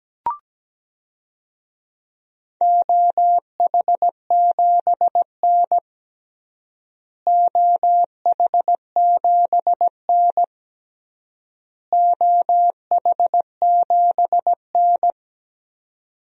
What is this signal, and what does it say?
Morse code, heard as a keyed tone, about 17 wpm, 700 Hz. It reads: OH7N OH7N OH7N
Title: DXCC-100 Call Signs - 4 Characters - Repeated 3x - 17wpm